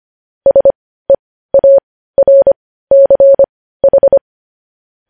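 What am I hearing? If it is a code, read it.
Morse code, heard as a keyed tone, about 25 wpm, spaced as for 15 wpm, 550 Hz. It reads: SEARCH
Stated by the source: News headlines sent in Morse code at 15 wpm.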